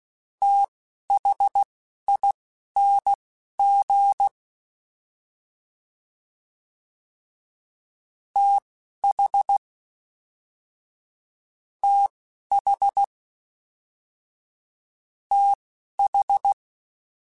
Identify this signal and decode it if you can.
Morse code, heard as a keyed tone, about 16 wpm, spaced as for 12 wpm, 800 Hz. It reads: THING TH TH TH
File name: Combinations / Combinations-mp3s / Combinations@10